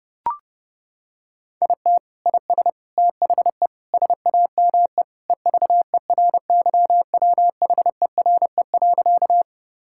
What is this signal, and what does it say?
Morse code, heard as a keyed tone, about 30 wpm, 700 Hz. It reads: IT IS THE SAME EVERYWHERE.
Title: Sentences from Top 1600 Words 30wpm